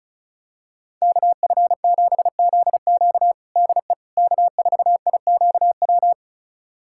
Morse code, heard as a keyed tone, about 36 wpm, 700 Hz. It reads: KF7ZQ DE K4IQW
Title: USA Callsign DE Callsign - 5 Characters - 35wpm